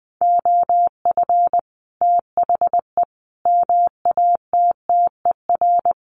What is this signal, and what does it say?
Morse code, heard as a keyed tone, about 20 wpm, 700 Hz. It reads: OF THE MATTER
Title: Sets of 5 Words Encore 20wpm